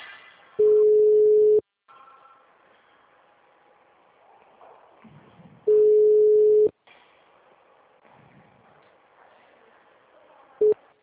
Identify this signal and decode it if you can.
Ringback tone (Europe, 425 Hz).